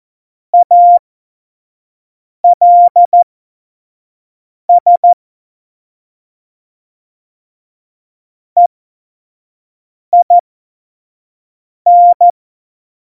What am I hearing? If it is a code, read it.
Morse code, heard as a keyed tone, about 14 wpm, spaced as for 5 wpm, 700 Hz. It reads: ALS EIN
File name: Morsekurs / Witze / Witz 19. Februar 2019, 25 F70